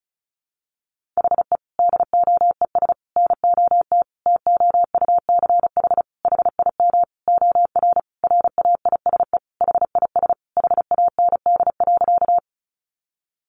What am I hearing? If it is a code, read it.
Morse code, heard as a keyed tone, about 34 wpm, 700 Hz. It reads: HE DOES NOT TOUCH HIM OR RAISE HIS HAND.
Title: Sentences from Top 1400 Words 35wpm